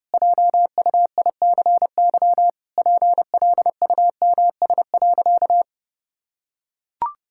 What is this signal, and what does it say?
Morse code, heard as a keyed tone, about 32 wpm, 700 Hz. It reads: JUICY PLUMS.
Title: Pangram Challenge v2 - 30wpm